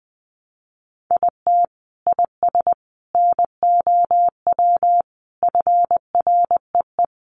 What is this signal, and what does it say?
Morse code, heard as a keyed tone, about 20 wpm, 700 Hz. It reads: IT IS NOW FREE
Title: Sentences from Top 1300 Words 20wpm